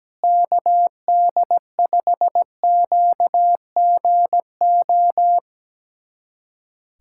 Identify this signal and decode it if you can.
Morse code, heard as a keyed tone, about 17 wpm, 700 Hz. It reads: KD5QGO
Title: USA Callsign DE Callsign 17wpm